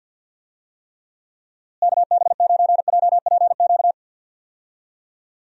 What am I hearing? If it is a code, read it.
Morse code, heard as a keyed tone, about 52 wpm, 700 Hz. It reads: KB9JPQ
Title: USA Call Signs - 6 Characters - Repeated 3x - 50wpm